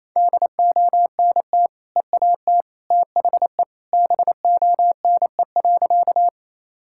Morse code, heard as a keyed tone, about 28 wpm, 700 Hz. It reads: DONT EAT THE BONE.
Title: Sentences from Top 1600 Words 28wpm